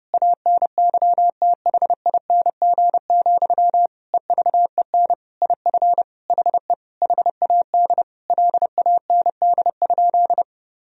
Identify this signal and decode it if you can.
Morse code, heard as a keyed tone, about 30 wpm, 700 Hz. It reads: ANYTHING, EVEN IF HE HAD LAND?
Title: Sentences from Top 1000 Words - Review Part 1 - 30wpm